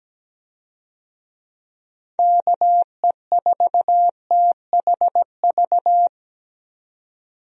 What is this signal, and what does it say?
Morse code, heard as a keyed tone, about 17 wpm, 700 Hz. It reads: KE4THV
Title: USA Call Signs - 6 Characters - 17wpm